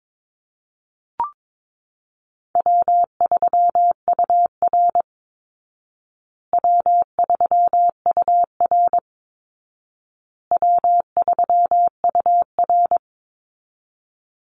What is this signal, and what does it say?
Morse code, heard as a keyed tone, about 22 wpm, 700 Hz. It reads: W3UR W3UR W3UR
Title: Easy Call Signs Repeated 3x - 22wpm